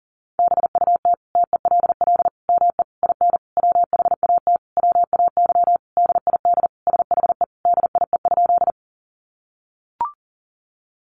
Morse code, heard as a keyed tone, about 40 wpm, 700 Hz. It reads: BUT TELL ME IN WHAT WAY DID SHE DIE?
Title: Sentences from Top 1300 Words 40wpm